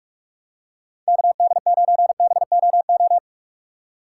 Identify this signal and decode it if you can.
Morse code, heard as a keyed tone, about 42 wpm, 700 Hz. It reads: KD9BOO
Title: USA Call Signs - 3x Character Spacing - 45wpm